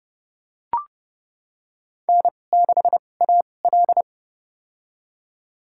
Morse code, heard as a keyed tone, about 32 wpm, spaced as for 22 wpm, 700 Hz. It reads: N6AL
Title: USA Call Signs - 2x Character Spacing - 30wpm